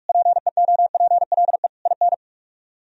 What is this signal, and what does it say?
Morse code, heard as a keyed tone, about 49 wpm, 700 Hz. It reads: PEOPLE IN